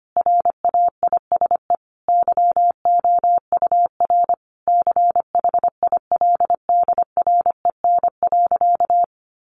Morse code, heard as a keyed tone, about 25 wpm, 700 Hz. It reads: RAISE YOUR CHILDREN.
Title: Sentences from Top 1400 Words 25wpm